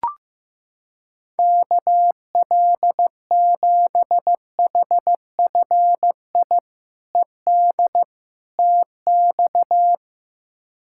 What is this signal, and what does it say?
Morse code, heard as a keyed tone, about 15 wpm, 700 Hz. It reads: KL7HFI ED TX
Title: K1USN SST QSO Practice - 15wpm